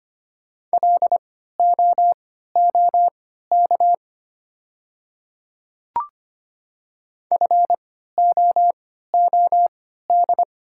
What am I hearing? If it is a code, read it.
Morse code, heard as a keyed tone, about 25 wpm, spaced as for 14 wpm, 700 Hz. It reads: LOOK FOOD
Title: 3x Character Spacing - Top 300 Words - 25wpm